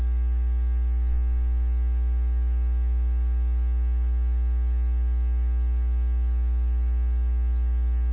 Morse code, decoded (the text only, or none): none